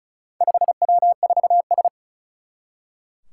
Morse code, heard as a keyed tone, about 36 wpm, 700 Hz. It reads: 5W4S
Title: Four Challenging Letter-Number Combinations 35wpm